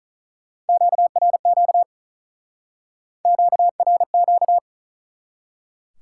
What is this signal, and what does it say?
Morse code, heard as a keyed tone, about 48 wpm, 700 Hz. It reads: QRQ QRQ